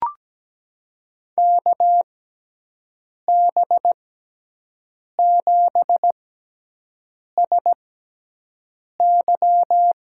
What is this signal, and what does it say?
Morse code, heard as a keyed tone, about 17 wpm, spaced as for 6 wpm, 700 Hz. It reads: KB7SY
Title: USA Call Signs - 6 Characters - 6x Character Spacing - 17wpm